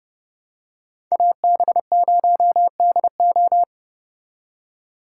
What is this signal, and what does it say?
Morse code, heard as a keyed tone, about 29 wpm, 700 Hz. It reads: AB0DO